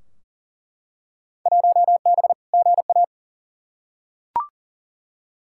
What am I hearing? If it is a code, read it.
Morse code, heard as a keyed tone, about 40 wpm, 700 Hz. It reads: 1B GA
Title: Field Day Exchanges - 40wpm